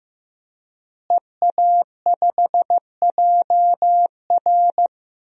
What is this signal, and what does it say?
Morse code, heard as a keyed tone, about 16 wpm, 700 Hz. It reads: EA5JR
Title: DXCC-100 Call Signs - 5 Characters - Repeated 2x - 15wpm